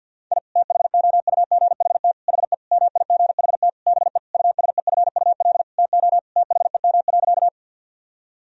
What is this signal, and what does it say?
Morse code, heard as a keyed tone, about 50 wpm, 700 Hz. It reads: I THOUGHT HE MIGHT BE USEFUL TO THEM.
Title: Sentences from Top 1000 Words - Review Part 4 - 50wpm